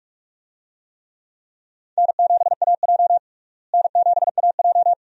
Morse code, heard as a keyed tone, about 46 wpm, 700 Hz. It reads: N7AJ N7AJ
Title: USA Call Signs - 4 Characters - Repeated 2x - 45wpm